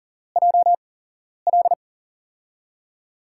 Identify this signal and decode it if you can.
Morse code, heard as a keyed tone, about 40 wpm, spaced as for 11 wpm, 700 Hz. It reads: JL
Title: USA Call Signs - 5 Characters - 8x Character Spacing - 40wpm